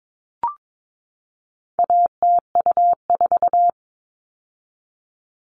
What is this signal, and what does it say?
Morse code, heard as a keyed tone, about 22 wpm, 700 Hz. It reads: ATU4
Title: All Cut Numbers - 4-Digit 22wpm